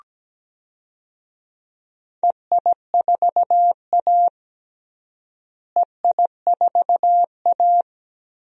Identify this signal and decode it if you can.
Morse code, heard as a keyed tone, about 17 wpm, 700 Hz. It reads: EI4A EI4A
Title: DXCC-100 Call Signs - 4 Characters - Repeated 2x - 17wpm